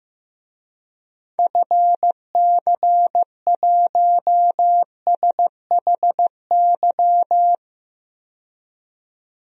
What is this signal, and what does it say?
Morse code, heard as a keyed tone, about 15 wpm, 700 Hz. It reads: FC1SHY
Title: DXCC-100 Call Signs - 6 Characters - 15wpm